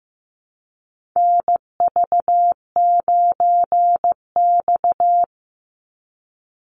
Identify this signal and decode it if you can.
Morse code, heard as a keyed tone, about 15 wpm, 700 Hz. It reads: NV9X
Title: Call Signs Repeated 2x - 15wpm